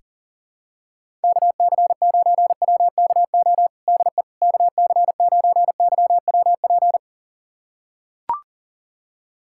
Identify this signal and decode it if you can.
Morse code, heard as a keyed tone, about 40 wpm, 700 Hz. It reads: KC9WKO DE KC9YWP
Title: USA Callsign DE Callsign 40wpm